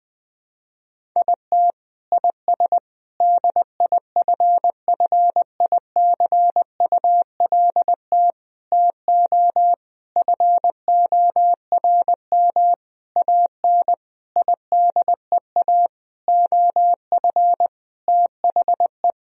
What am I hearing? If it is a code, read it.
Morse code, heard as a keyed tone, about 20 wpm, 700 Hz. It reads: IT IS DIFFICULT TO FORM AN IDEA OF THE
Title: Sentences from Top 1000 Words - Review Part 1 - 20wpm